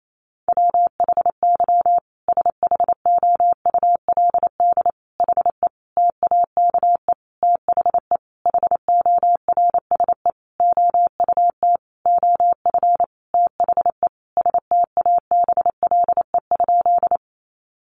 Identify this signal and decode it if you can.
Morse code, heard as a keyed tone, about 28 wpm, 700 Hz. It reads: WHY SHOULD HE TAKE THE HORSE OUT OF THE STABLE?
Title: Sentences from Top 1500 Words 28wpm